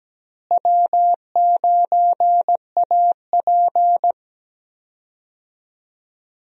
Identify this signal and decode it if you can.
Morse code, heard as a keyed tone, about 17 wpm, 700 Hz. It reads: W9AP
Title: USA Call Signs - 4 Characters - Repeated 3x - 17wpm